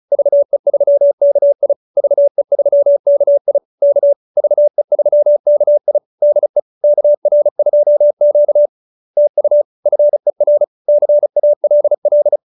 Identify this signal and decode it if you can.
Morse code, heard as a keyed tone, about 34 wpm, 590 Hz. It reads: VE3KI VE3KI K VE3KI DE KR2Q TU FER CALL